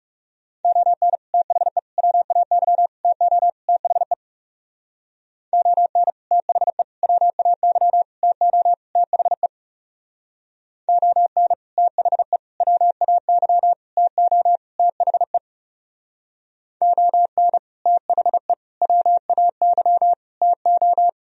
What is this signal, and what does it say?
Morse code, heard as a keyed tone, about 36 wpm, 700 Hz. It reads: ON THE WAY TO THE ON THE WAY TO THE ON THE WAY TO THE ON THE WAY TO
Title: Speed-Racing - Sets of 5 Words 30wpm